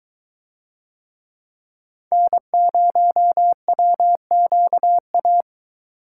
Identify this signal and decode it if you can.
Morse code, heard as a keyed tone, about 24 wpm, 700 Hz. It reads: N0WQA